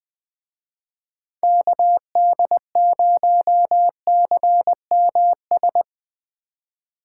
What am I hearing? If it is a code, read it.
Morse code, heard as a keyed tone, about 20 wpm, 700 Hz. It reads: KD0CMS